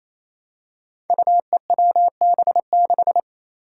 Morse code, heard as a keyed tone, about 27 wpm, 700 Hz. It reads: UEWB6
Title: Five Challenging Letter-Number Combinations 28wpm